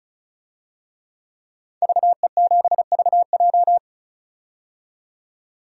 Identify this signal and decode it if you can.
Morse code, heard as a keyed tone, about 34 wpm, 700 Hz. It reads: VE7VJ
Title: DXCC-100 Call Signs - 5 Characters - 35wpm